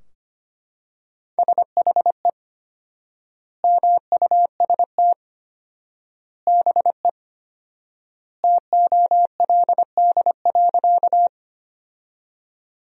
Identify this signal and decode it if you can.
Morse code, heard as a keyed tone, about 25 wpm, 700 Hz. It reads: SHE MUST BE TOLD.
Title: 4x Word-Spacing - Sentences from Top 400 Words - 25wpm